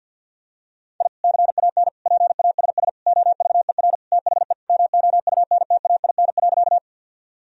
Morse code, heard as a keyed tone, about 46 wpm, 700 Hz. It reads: I CAN PASS OVER THE MOUNTAIN.
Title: Sentences from Top 1000 Words - Review Part 1 - 50wpm